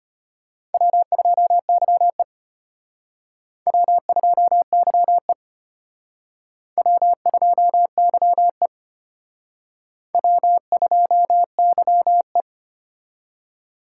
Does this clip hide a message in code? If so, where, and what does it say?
Yes: Morse code from 0.7 to 12.4 s, heard as a keyed tone, about 32 wpm, 700 Hz. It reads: W2YE W2YE W2YE W2YE